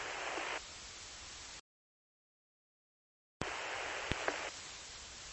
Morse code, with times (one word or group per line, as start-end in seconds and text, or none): none